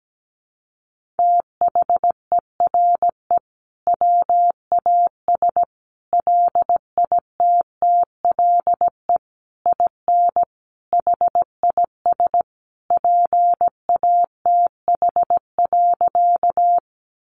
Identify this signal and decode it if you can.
Morse code, heard as a keyed tone, about 17 wpm, 700 Hz. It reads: THERE WAS LITTLE IN HIS PATH.
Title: Sentences from Top 1400 Words 17wpm